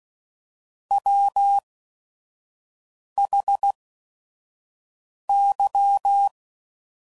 Morse code, heard as a keyed tone, about 16 wpm, spaced as for 5 wpm, 800 Hz. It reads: WHY
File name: CW Code Groups @5WPM / Technical Subjects / OR Gate